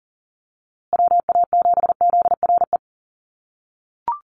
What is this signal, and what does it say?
Morse code, heard as a keyed tone, about 40 wpm, 700 Hz. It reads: WA7ZRE
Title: USA Call Signs Repeated 2x 40wpm